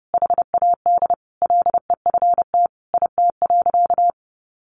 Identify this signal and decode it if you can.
Morse code, heard as a keyed tone, about 30 wpm, 700 Hz. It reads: HAD LEFT IT.
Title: A Sound Of Thunder 30wpm